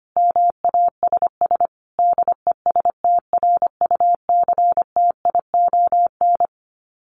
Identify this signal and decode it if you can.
Morse code, heard as a keyed tone, about 25 wpm, 700 Hz. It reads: MASS DESTRUCTION
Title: Sets of 5 Words Encore 25wpm